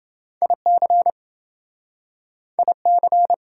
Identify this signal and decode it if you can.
Morse code, heard as a keyed tone, about 29 wpm, 700 Hz. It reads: IC IC